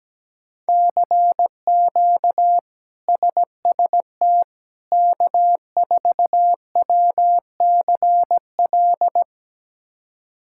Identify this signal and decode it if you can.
Morse code, heard as a keyed tone, about 17 wpm, 700 Hz. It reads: CQ SST K4WCL
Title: K1USN SST QSO Practice - 17wpm